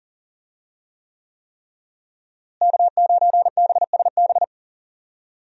Morse code, heard as a keyed tone, about 40 wpm, 700 Hz. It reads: K9BSB
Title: USA Call Signs - 5 Characters - Repeated 3x - 40wpm